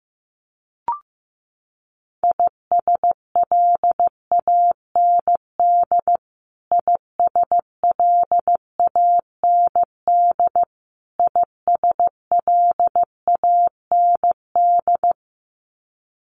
Morse code, heard as a keyed tone, about 14 wpm, 700 Hz. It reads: ISLAND ISLAND ISLAND
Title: Top 500 Words Repeated 15wpm